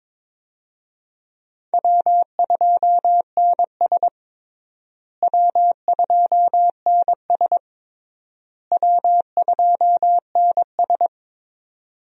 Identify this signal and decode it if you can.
Morse code, heard as a keyed tone, about 21 wpm, 700 Hz. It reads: W2NS W2NS W2NS
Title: USA Call Signs - 4 Characters - Repeated 3x - 22wpm